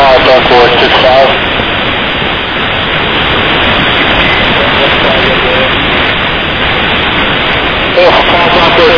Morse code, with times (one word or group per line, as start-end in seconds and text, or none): none